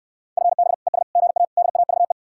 Morse code, heard as a keyed tone, about 69 wpm, 700 Hz. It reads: FB ES GA DAVE